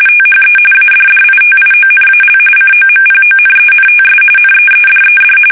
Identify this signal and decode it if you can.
RTTY, about 45.45 baud. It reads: THE ADDITIONAL COMPONENTS NEEDED
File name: fsk850 / teletype-corp